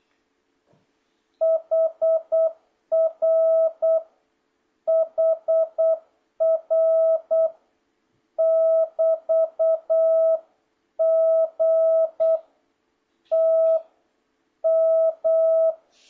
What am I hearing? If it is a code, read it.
Morse code, heard as a keyed tone, about 8 wpm, 650 Hz. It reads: HR HR <BT> G T M